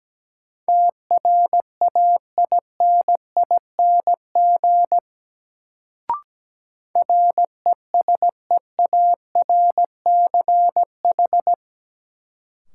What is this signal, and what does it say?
Morse code, heard as a keyed tone, about 17 wpm, 700 Hz. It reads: TRAINING RESEARCH
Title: Sentences from Top 1000 Words - Review Part 4 - New Word Drill - 17wpm